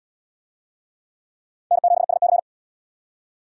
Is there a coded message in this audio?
Yes: Morse code from 1.7 to 2.4 s, heard as a keyed tone, about 70 wpm, 700 Hz. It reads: N7SC